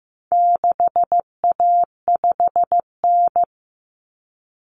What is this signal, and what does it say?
Morse code, heard as a keyed tone, about 15 wpm, 700 Hz. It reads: 6A5N
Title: Common Cut Numbers - 4-Digit 15wpm